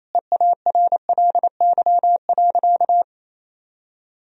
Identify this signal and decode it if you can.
Morse code, heard as a keyed tone, about 28 wpm, 700 Hz. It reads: EARLY.
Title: Sentences from Top 1000 Words - Review Part 2 - 28wpm